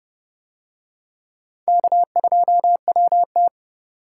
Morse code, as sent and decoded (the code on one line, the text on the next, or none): -.- ..--- .-- -
K2WT